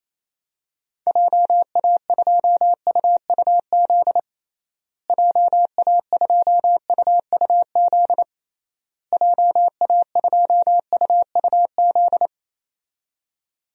Morse code, heard as a keyed tone, about 27 wpm, 700 Hz. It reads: JA2UUZ JA2UUZ JA2UUZ
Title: DXCC-100 Call Signs - 6 Characters - Repeated 3x - 28wpm